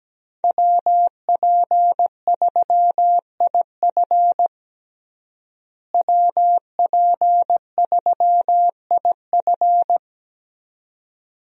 Morse code, heard as a keyed tone, about 17 wpm, 700 Hz. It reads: WP3IF WP3IF
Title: USA Call Signs - 5 Characters - Repeated 2x - 17wpm